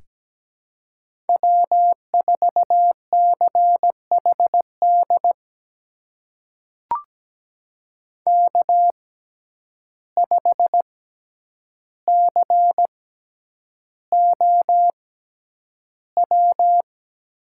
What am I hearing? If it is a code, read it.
Morse code, heard as a keyed tone, about 17 wpm, spaced as for 9 wpm, 700 Hz. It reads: W4CHD K5COW
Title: USA Call Signs - 5 Characters - 6x Character Spacing - 17wpm